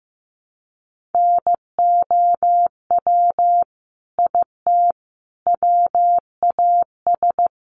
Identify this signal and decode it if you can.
Morse code, heard as a keyed tone, about 15 wpm, 700 Hz. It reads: NOW IT WAS
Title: Sentences from Top 200 Words 15wpm